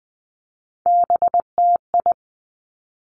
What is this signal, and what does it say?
Morse code, heard as a keyed tone, about 20 wpm, 700 Hz. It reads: BTI
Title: Three Challenging Letter-Number Combinations 20wpm